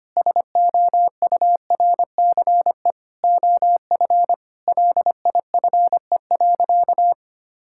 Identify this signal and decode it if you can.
Morse code, heard as a keyed tone, about 25 wpm, 700 Hz. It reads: SOURCE OF LIFE.
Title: Sentences from Top 1000 Words - Review Part 5 - 25wpm